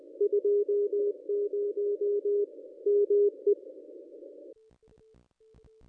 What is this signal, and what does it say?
Morse code, heard as a keyed tone, about 20 wpm, 410 Hz. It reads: 20 ME
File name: contests / ss-cw-k5tr-op-n5rz -1000